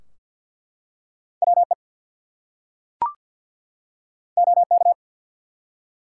Morse code, heard as a keyed tone, about 50 wpm, 700 Hz. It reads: WE OK
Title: Two Letter Words - 50wpm